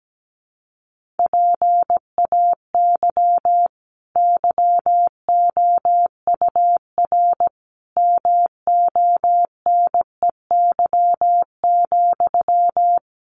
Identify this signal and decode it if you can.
Morse code, heard as a keyed tone, about 17 wpm, 700 Hz. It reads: PAY YOUR MONEY,
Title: Sentences from Top 900 Words 17wpm